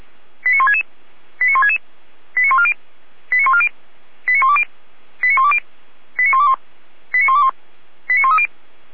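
Five-tone selective call (ZVEI), heard as quick runs of five tones, tones of about 70 ms. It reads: 89166 89166 89140 89140 89120 89120 89121 89121 89130
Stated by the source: ZVEI five-tone selective call received on SDR.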